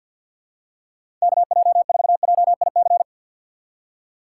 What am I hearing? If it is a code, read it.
Morse code, heard as a keyed tone, about 53 wpm, 700 Hz. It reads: KJ4JIC
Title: USA Call Signs - 4x Character Spacing - 50wpm